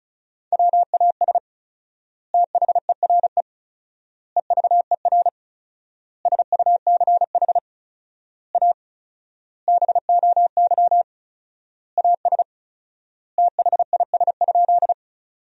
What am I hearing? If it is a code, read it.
Morse code, heard as a keyed tone, about 36 wpm, 700 Hz. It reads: WAS THERE EVER SUCH A BOY AS THIS?
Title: 4x Word-Spacing - Sentences from Top 400 Words - 35wpm